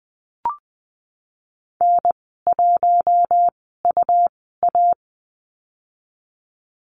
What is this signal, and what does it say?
Morse code, heard as a keyed tone, about 19 wpm, spaced as for 14 wpm, 700 Hz. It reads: N1UA
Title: 200 Lesson 32g - Letter k - Call Signs 20wpm